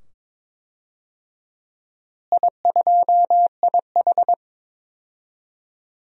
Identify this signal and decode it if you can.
Morse code, heard as a keyed tone, about 22 wpm, 700 Hz. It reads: I2IH
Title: DXCC-100 Call Signs - 4 Characters - Repeated 2x - 22wpm